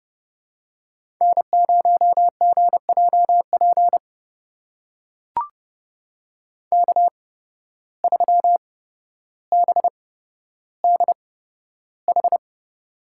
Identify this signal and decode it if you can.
Morse code, heard as a keyed tone, about 30 wpm, spaced as for 13 wpm, 700 Hz. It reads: N0GJP K3BDH
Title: USA Call Signs - 5 Characters - 8x Character Spacing - 30wpm